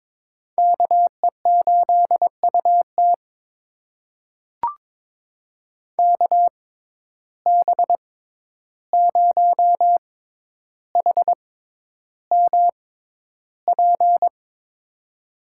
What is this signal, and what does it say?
Morse code, heard as a keyed tone, about 22 wpm, spaced as for 11 wpm, 700 Hz. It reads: KE8UT KB0HMP
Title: USA Call Signs - 6x Character Spacing - 22wpm